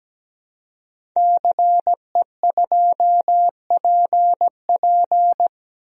Morse code, heard as a keyed tone, about 17 wpm, 700 Hz. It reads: CE2PP
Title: DXCC-100 Call Signs - 5 Characters - Repeated 3x - 17wpm